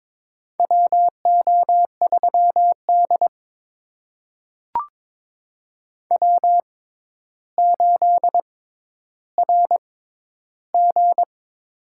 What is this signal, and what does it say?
Morse code, heard as a keyed tone, about 22 wpm, spaced as for 11 wpm, 700 Hz. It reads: WO3D W8RG
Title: USA Call Signs - 4 Characters - 6x Character Spacing - 22wpm